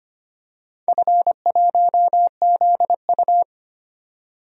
Morse code, heard as a keyed tone, about 25 wpm, 700 Hz. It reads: F1ZU
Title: DXCC-100 Call Signs - 4 Characters - 25wpm